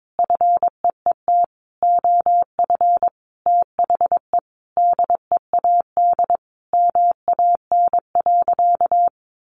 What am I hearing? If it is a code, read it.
Morse code, heard as a keyed tone, about 22 wpm, 700 Hz. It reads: FEET OF THE DEAD MAN.
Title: Sentences from Top 1500 Words 22wpm